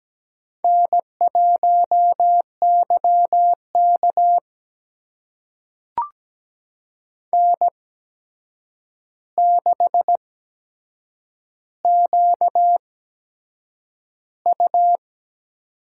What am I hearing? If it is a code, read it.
Morse code, heard as a keyed tone, about 17 wpm, spaced as for 7 wpm, 700 Hz. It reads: N1YK N6QU
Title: USA Call Signs - 4 Characters - 8x Character Spacing - 17wpm